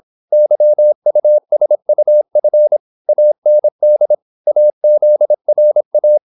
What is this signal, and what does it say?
Morse code, heard as a keyed tone, about 26 wpm, 600 Hz. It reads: YUSUF AND AZRA
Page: Never 26wpm | CW med Gnister